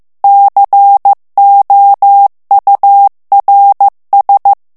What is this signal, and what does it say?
Morse code, heard as a keyed tone, about 15 wpm, 800 Hz. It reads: COURS
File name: morse mp3 / FRANCAIS TEXTE CLAIR 1 VITESSE 14 MOTS MINUTES